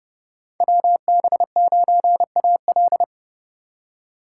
Morse code, heard as a keyed tone, about 32 wpm, 700 Hz. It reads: WB9AL
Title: USA Call Signs - 5 Characters - 2x Character Spacing - 30wpm